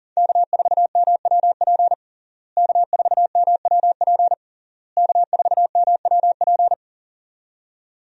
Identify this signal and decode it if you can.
Morse code, heard as a keyed tone, about 40 wpm, 700 Hz. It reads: K4MWP K4MWP K4MWP
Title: USA Call Signs - 5 Characters - Repeated 3x - 40wpm